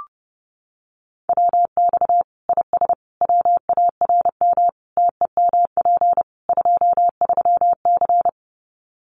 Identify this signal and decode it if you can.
Morse code, heard as a keyed tone, about 30 wpm, 700 Hz. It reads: WX IS WARM TEMP 23C
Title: Common QSO Phrases 30wpm